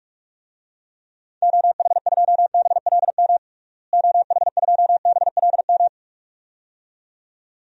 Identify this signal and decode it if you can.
Morse code, heard as a keyed tone, about 46 wpm, 700 Hz. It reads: OH2BLM OH2BLM
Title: DXCC-100 Call Signs - 6 Characters - Repeated 2x - 45wpm